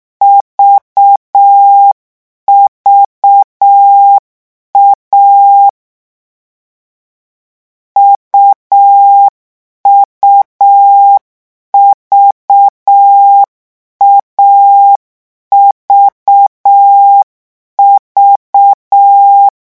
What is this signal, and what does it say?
Morse code, heard as a keyed tone, about 6 wpm, 800 Hz. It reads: VVA UUV AVV
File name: morse mp3 / lecon3-vitesse 6